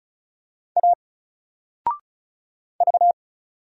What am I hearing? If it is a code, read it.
Morse code, heard as a keyed tone, about 36 wpm, 700 Hz. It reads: A V
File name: Single-Letter-Number-35wpm